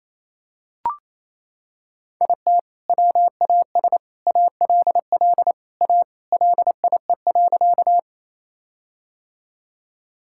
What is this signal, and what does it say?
Morse code, heard as a keyed tone, about 27 wpm, 700 Hz. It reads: IT WAS ALL A LIE.
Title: Sentences from Top 1600 Words 28wpm